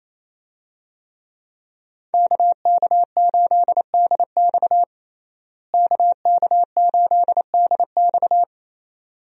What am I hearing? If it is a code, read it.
Morse code, heard as a keyed tone, about 28 wpm, 700 Hz. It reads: KK8DX KK8DX
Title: USA Call Signs - 5 Characters - Repeated 2x - 28wpm